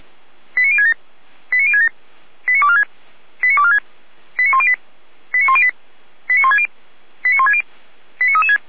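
Five-tone selective call (ZVEI), heard as quick runs of five tones, tones of about 70 ms. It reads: 89067 89067 89246 89246 89108 89108 89160 89160 89336
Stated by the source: ZVEI five-tone call received by SDR.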